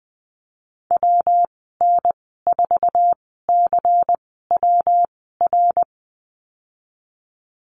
Morse code, heard as a keyed tone, about 20 wpm, spaced as for 14 wpm, 700 Hz. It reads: WN4CWR
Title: 207 Lesson 33g - Letter j - Call Signs 20wpm